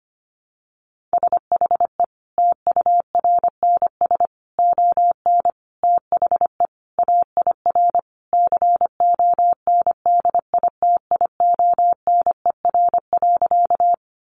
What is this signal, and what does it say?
Morse code, heard as a keyed tone, about 25 wpm, 700 Hz. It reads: SHE TURNS ON THE AIR CONDITIONER.
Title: Additional Easy Phrases 25wpm